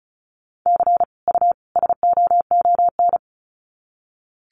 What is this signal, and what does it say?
Morse code, heard as a keyed tone, about 34 wpm, 700 Hz. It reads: C U SOON
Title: Common QSO Phrases 35wpm